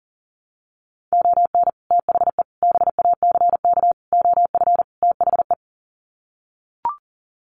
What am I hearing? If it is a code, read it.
Morse code, heard as a keyed tone, about 40 wpm, 700 Hz. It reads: ON THE BACK OF THE
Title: Sets of 5 Words 40wpm